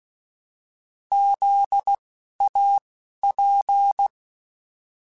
Morse code, heard as a keyed tone, about 16 wpm, spaced as for 11 wpm, 800 Hz. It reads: ZAP
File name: Alphabetical / Alphabetical mp3s / Zs @10